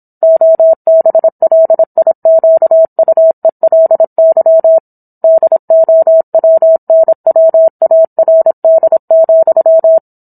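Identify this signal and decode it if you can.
Morse code, heard as a keyed tone, about 27 wpm, 650 Hz. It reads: OBLIQUELY DOWNWARD,
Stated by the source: Morse code practice of a book chapter at 26 WPM.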